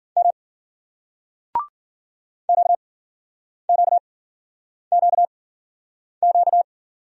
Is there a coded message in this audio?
Yes: Morse code from 0.2 to 6.6 s, heard as a keyed tone, about 53 wpm, spaced as for 9 wpm, 700 Hz. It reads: M QQQQ